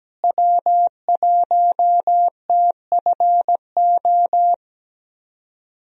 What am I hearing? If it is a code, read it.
Morse code, heard as a keyed tone, about 17 wpm, 700 Hz. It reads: W1TFO
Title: USA Call Signs - 5 Characters - Repeated 2x - 17wpm